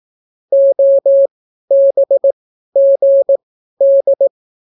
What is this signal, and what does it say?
Morse code, heard as a keyed tone, about 18 wpm, spaced as for 12 wpm, 550 Hz. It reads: OBGD